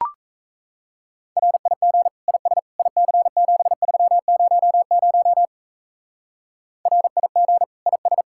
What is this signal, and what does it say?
Morse code, heard as a keyed tone, about 35 wpm, 700 Hz. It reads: RIG IS IC7300 RIG IS
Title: Speed-Racing Common QSO Phrases - 28wpm